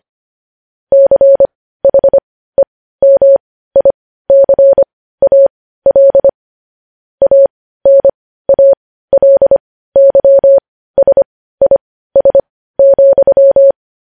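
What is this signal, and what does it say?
Morse code, heard as a keyed tone, about 25 wpm, spaced as for 15 wpm, 550 Hz. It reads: CHEMICAL ANALYSIS,
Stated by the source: Morse news headlines at 15 wpm.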